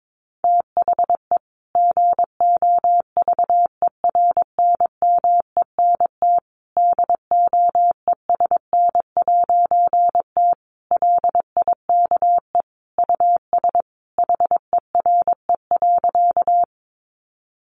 Morse code, heard as a keyed tone, about 22 wpm, 700 Hz. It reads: THE GOVERNMENT DOESN'T LIKE US HERE.
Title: A Sound Of Thunder 22wpm